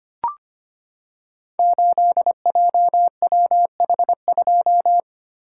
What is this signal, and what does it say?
Morse code, heard as a keyed tone, about 25 wpm, 700 Hz. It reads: 8JWH2
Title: Code Groups - 5 Letter:Numbers - 25wpm